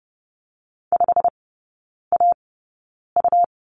Morse code, heard as a keyed tone, about 30 wpm, spaced as for 9 wpm, 700 Hz. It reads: 5AU
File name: Lesson 17e - Letter u - Three Characters 30wpm